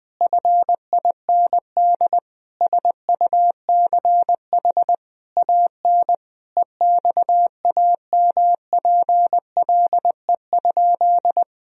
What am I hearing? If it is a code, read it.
Morse code, heard as a keyed tone, about 20 wpm, 700 Hz. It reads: FIND SUCH AN EXAMPLE?
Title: Sentences from Top 1000 Words - Review Part 1 - 20wpm